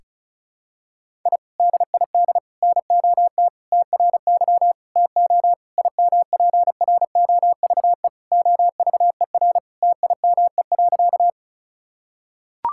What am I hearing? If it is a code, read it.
Morse code, heard as a keyed tone, about 34 wpm, 700 Hz. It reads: I DID NOT TRY TO IMPROVE OVER TIME.
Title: Sentences from Top 1000 Words - Review Part 5 - 35wpm